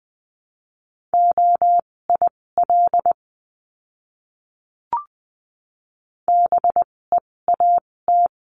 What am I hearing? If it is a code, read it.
Morse code, heard as a keyed tone, about 20 wpm, spaced as for 16 wpm, 700 Hz. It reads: OIL BEAT
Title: 256 Lesson 42a - Words 12-20wpm